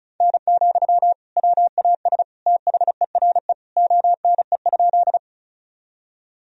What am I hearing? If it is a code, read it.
Morse code, heard as a keyed tone, about 36 wpm, 700 Hz. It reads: N, WAS THERE ONE?
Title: Sentences from Top 100 Words and Abbreviations 35wpm